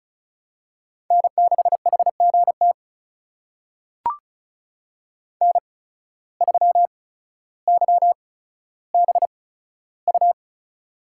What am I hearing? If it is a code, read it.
Morse code, heard as a keyed tone, about 34 wpm, spaced as for 15 wpm, 700 Hz. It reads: N6HGT N3YBU
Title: USA Call Signs - 5 Characters - 8x Character Spacing - 35wpm